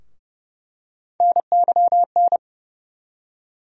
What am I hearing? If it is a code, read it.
Morse code, heard as a keyed tone, about 30 wpm, 700 Hz. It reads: NYN
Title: USA Call Sign Postfixes - 3 Letters 30wpm